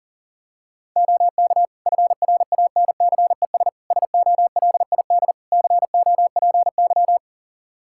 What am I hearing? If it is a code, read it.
Morse code, heard as a keyed tone, about 40 wpm, 700 Hz. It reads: OK FRANCES SOLID COPY
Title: Practice QSO - Exchange 3 - More Info - 40wpm